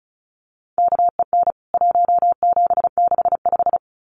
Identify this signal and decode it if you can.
Morse code, heard as a keyed tone, about 34 wpm, 700 Hz. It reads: KEN 1765
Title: CWT Exchange 35wpm